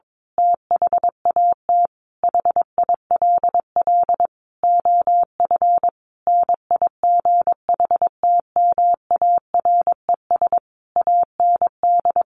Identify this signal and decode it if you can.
Morse code, heard as a keyed tone, about 22 wpm, 700 Hz. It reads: THAT HILL OF NIGHTMARES AND
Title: A Sound Of Thunder 22wpm